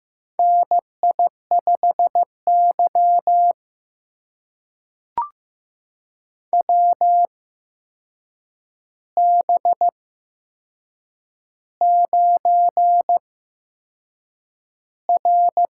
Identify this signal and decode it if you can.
Morse code, heard as a keyed tone, about 15 wpm, spaced as for 6 wpm, 700 Hz. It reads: NI5Y WB9R
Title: USA Call Signs - 8x Character Spacing - 15wpm